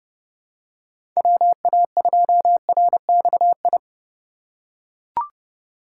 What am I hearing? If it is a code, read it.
Morse code, heard as a keyed tone, about 29 wpm, 700 Hz. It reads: WA2RXI
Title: USA Call Signs - 6 Characters - 4x Character Spacing - 30wpm